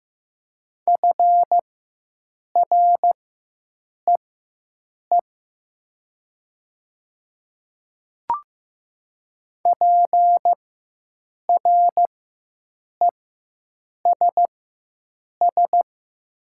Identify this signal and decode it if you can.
Morse code, heard as a keyed tone, about 15 wpm, spaced as for 7 wpm, 700 Hz. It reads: FREE PRESS
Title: 4x Character Spacing - Top 500 Words - 15wpm